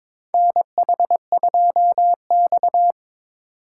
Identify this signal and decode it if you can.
Morse code, heard as a keyed tone, about 21 wpm, 700 Hz. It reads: NH2X
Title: USA Call Signs - 4x Character Spacing - 22wpm